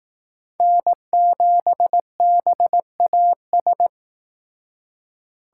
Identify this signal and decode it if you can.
Morse code, heard as a keyed tone, about 19 wpm, 700 Hz. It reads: N7BAS